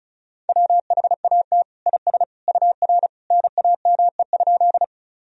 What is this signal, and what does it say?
Morse code, heard as a keyed tone, about 34 wpm, 690 Hz. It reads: WHAT IS UR NAME?
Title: Sentences from Top 200 Words and Abbreviations 35wpm